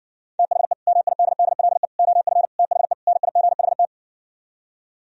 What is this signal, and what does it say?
Morse code, heard as a keyed tone, about 59 wpm, 700 Hz. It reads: THE MIDDLE OF THE NIGHT